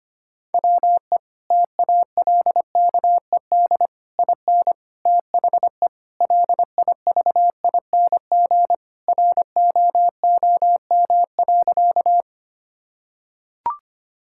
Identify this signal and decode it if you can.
Morse code, heard as a keyed tone, about 25 wpm, 700 Hz. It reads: WE TALKED IN THE LIVING ROOM.
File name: practice / 25wpm / Sentences with 4-Word Phrases - 25wpm